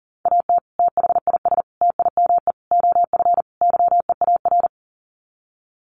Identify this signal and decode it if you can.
Morse code, heard as a keyed tone, about 40 wpm, 700 Hz. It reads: AT THIS TIME OF YEAR
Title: Sets of 5 Words Encore2 40wpm